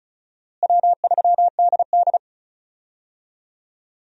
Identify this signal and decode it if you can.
Morse code, heard as a keyed tone, about 36 wpm, 700 Hz. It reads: W3DD
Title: USA Call Signs - 4 Characters - Repeated 2x - 35wpm